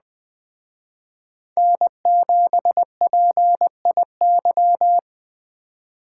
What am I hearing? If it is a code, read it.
Morse code, heard as a keyed tone, about 20 wpm, 700 Hz. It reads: N7PIY
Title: Speed Racing - USA 1x3 Call Signs 17wpm